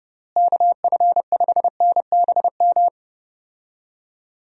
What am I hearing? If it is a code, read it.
Morse code, heard as a keyed tone, about 30 wpm, 700 Hz. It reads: KF5NBM
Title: USA Call Signs - 6 Characters - Repeated 2x - 30wpm